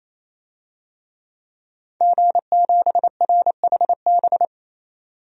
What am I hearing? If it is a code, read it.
Morse code, heard as a keyed tone, about 28 wpm, 700 Hz. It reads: G7RHB
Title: DXCC-100 Call Signs - 5 Characters - 28wpm